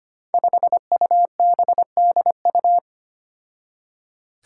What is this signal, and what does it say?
Morse code, heard as a keyed tone, about 25 wpm, 700 Hz. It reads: 5UBDU
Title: Five Challenging Letter-Number Combinations 25wpm